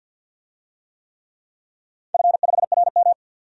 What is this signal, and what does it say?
Morse code, heard as a keyed tone, about 53 wpm, 700 Hz. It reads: F5RM
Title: DXCC-100 Call Signs - 4 Characters - 50wpm